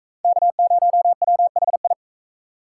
Morse code, heard as a keyed tone, about 39 wpm, 700 Hz. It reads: K0WHI